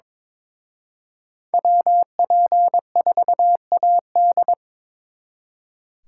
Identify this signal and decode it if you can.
Morse code, heard as a keyed tone, about 26 wpm, 700 Hz. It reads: WP4AD
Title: Speed Racing - USA 2x2 Call Signs 22wpm